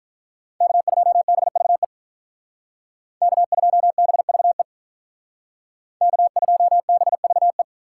Morse code, heard as a keyed tone, about 45 wpm, 700 Hz. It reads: K2BVE K2BVE K2BVE